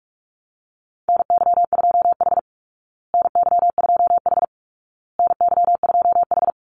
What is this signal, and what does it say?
Morse code, heard as a keyed tone, about 44 wpm, 700 Hz. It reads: NY2H NY2H NY2H
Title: Call Signs Repeated 2x - 45wpm